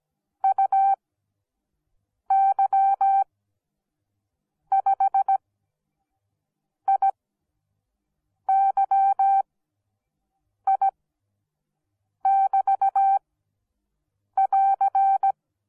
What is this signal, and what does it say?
Morse code, heard as a keyed tone, about 17 wpm, spaced as for 6 wpm, 780 Hz. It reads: UY5IYI<BT><AR>